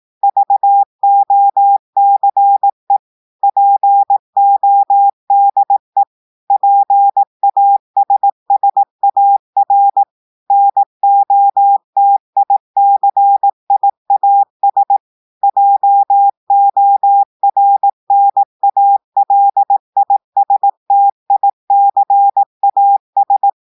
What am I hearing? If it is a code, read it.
Morse code, heard as a keyed tone, about 18 wpm, 800 Hz. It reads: VOCE PODE PASSAR NOTICIAS JORNALISTICAS
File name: audios / portugais / portugais texte en clair 2 vitesse 18 mots minutes